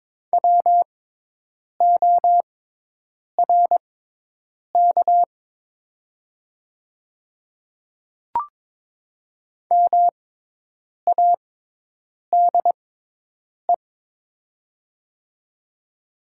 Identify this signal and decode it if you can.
Morse code, heard as a keyed tone, about 22 wpm, spaced as for 8 wpm, 700 Hz. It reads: WORK MADE
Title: 6x Character Spacing - Top 300 Words - 22wpm